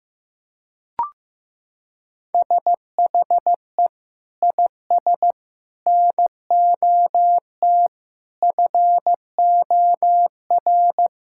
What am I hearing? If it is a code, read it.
Morse code, heard as a keyed tone, about 15 wpm, 700 Hz. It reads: SHE IS NOT FOR
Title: Sentences from Top 1000 Words - Review Part 5 - 15wpm